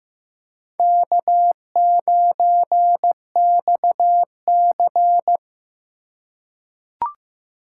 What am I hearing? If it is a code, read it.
Morse code, heard as a keyed tone, about 15 wpm, 700 Hz. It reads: K9XC
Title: USA Call Signs - 4x Character Spacing - 15wpm